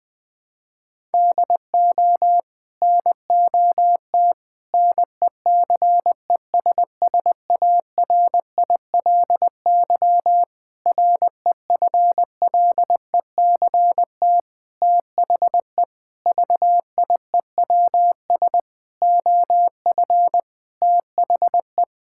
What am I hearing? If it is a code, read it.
Morse code, heard as a keyed tone, about 20 wpm, 700 Hz. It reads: DO NOT NECESSARILY REFLECT THE VIEWS OF THE
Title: Sets of 8 Words - 20wpm